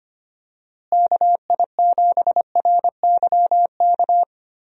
Morse code, heard as a keyed tone, about 25 wpm, 700 Hz. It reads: KI7RYK
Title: USA Call Signs - 6 Characters - Repeated 2x - 25wpm